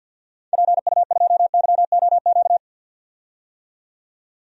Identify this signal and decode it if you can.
Morse code, heard as a keyed tone, about 51 wpm, 700 Hz. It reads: PU2YOQ
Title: DXCC-100 Call Signs - 6 Characters - 50wpm